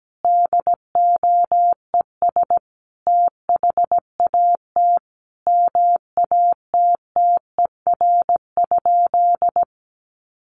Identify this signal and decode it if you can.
Morse code, heard as a keyed tone, about 17 wpm, 700 Hz. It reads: DOES THAT MATTER?
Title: Sentences from Top 700 Words 17wpm